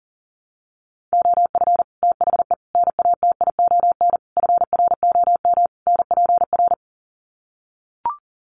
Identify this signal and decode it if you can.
Morse code, heard as a keyed tone, about 40 wpm, 700 Hz. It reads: OF THE NATION FROM NPR
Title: Sets of 5 Words 40wpm